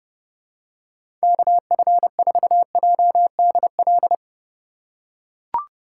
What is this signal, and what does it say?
Morse code, heard as a keyed tone, about 30 wpm, 700 Hz. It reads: KF4JDL